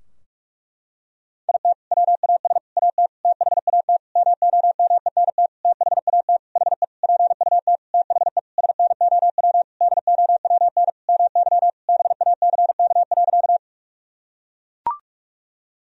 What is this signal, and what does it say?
Morse code, heard as a keyed tone, about 42 wpm, 700 Hz. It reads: IT WAS AT THAT MOMENT THAT HE PUT THE SNOW DOWN MY BACK.
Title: Sentences from Top 1000 Words - Review Part 3 - 45wpm